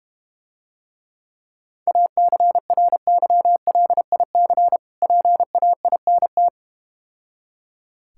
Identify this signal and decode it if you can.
Morse code, heard as a keyed tone, about 31 wpm, 700 Hz. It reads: ACRYLIC PAINT